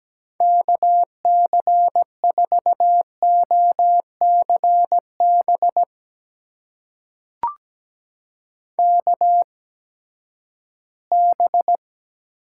Morse code, heard as a keyed tone, about 17 wpm, spaced as for 12 wpm, 700 Hz. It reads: KC4OCB KB
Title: USA Call Signs - 6 Characters - 8x Character Spacing - 17wpm